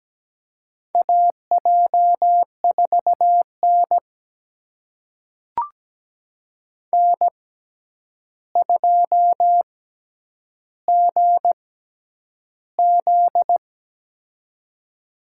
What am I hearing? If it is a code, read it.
Morse code, heard as a keyed tone, about 17 wpm, spaced as for 9 wpm, 700 Hz. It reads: AJ4N N2GZ
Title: USA Call Signs - 6x Character Spacing - 17wpm